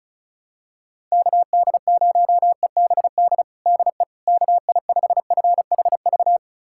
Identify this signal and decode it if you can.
Morse code, heard as a keyed tone, about 34 wpm, 690 Hz. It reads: KD0EBD DE KI5FHV
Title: USA Callsign DE Callsign - 6 Characters - 35wpm